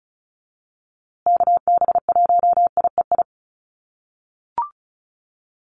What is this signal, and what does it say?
Morse code, heard as a keyed tone, about 34 wpm, 700 Hz. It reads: KB1IEI
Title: USA Call Signs Repeated 2x 35wpm